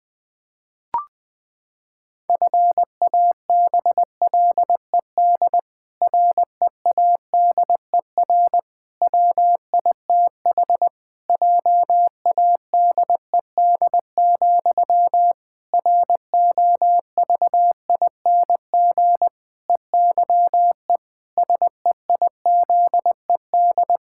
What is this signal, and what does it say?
Morse code, heard as a keyed tone, about 20 wpm, 700 Hz. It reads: FABLED READER WITH JADED, ROVING EYE SEIZED
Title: Pangram Challenge v2 - 20wpm